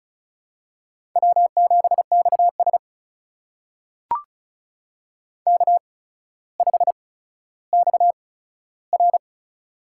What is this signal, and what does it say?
Morse code, heard as a keyed tone, about 36 wpm, spaced as for 15 wpm, 700 Hz. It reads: W7XS K5XR
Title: USA Call Signs - 4 Characters - 8x Character Spacing - 35wpm